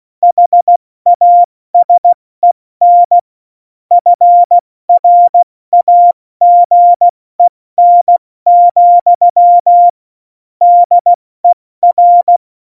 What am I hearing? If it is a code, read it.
Morse code, heard as a keyed tone, about 16 wpm, 700 Hz. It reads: HASEN FRAGEN, DER
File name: Morsekurs / Witze / Witz 08. Dezember 2020, 70 F80